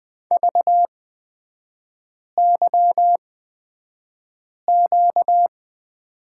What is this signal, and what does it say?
Morse code, heard as a keyed tone, about 20 wpm, spaced as for 5 wpm, 700 Hz. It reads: VYQ
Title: 250 Lesson 40e - BK - Three Characters 20wpm